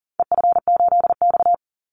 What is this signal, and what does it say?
Morse code, heard as a keyed tone, about 38 wpm, 700 Hz. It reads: EF8X
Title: Call Signs - 40wpm